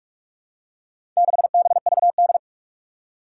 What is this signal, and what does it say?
Morse code, heard as a keyed tone, about 46 wpm, 700 Hz. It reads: 6BVD
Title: Four Challenging Letter-Number Combinations 45wpm